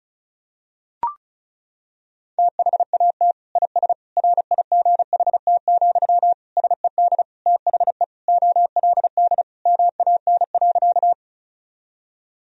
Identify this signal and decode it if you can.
Morse code, heard as a keyed tone, about 36 wpm, 700 Hz. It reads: THAT IS RIGHT, SED THE OLD MAN.
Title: Sentences from Top 200 Words and Abbreviations 35wpm